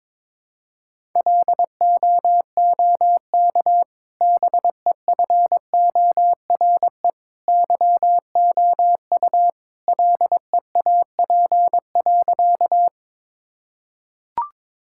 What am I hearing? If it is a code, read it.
Morse code, heard as a keyed tone, about 22 wpm, 700 Hz. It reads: LOOK BEFORE YOU LEAP.
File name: practice / 22wpm / Easy Sentences with Hints - 22wpm